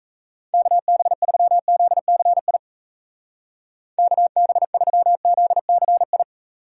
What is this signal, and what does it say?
Morse code, heard as a keyed tone, about 34 wpm, 700 Hz. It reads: KB3ZCI KB3ZCI
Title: Speed Racing - USA 2x3 Call Signs 28wpm